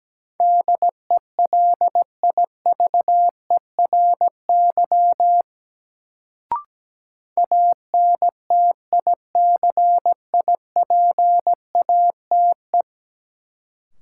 Morse code, heard as a keyed tone, about 17 wpm, 700 Hz. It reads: DELIVERY ANTICIPATE
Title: Sentences from Top 1500 Words - New Word Drill Alt - 17wpm